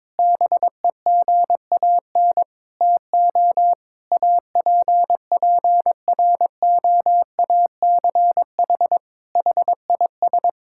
Morse code, heard as a keyed tone, about 22 wpm, 700 Hz. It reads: BEGAN TO APPROACH HIS
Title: Sentences from Top 1000 Words - Review Part 6 - 22wpm